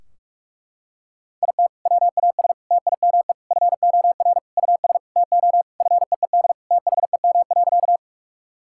Morse code, heard as a keyed tone, about 41 wpm, 700 Hz. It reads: IT WAS TIME FOR US TO FEED THEM.
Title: Sentences from Top 1600 Words 45wpm